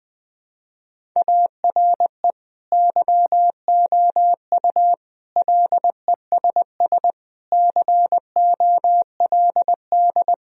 Morse code, heard as a keyed tone, about 20 wpm, 700 Hz. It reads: ARE YOU LESS COLD
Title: Sentences from Top 1000 Words - Review Part 2 - 20wpm